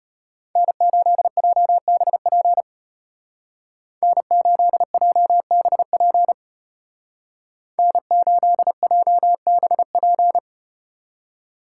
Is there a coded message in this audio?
Yes: Morse code from 0.6 to 10.4 s, heard as a keyed tone, about 34 wpm, 700 Hz. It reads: N8JBP N8JBP N8JBP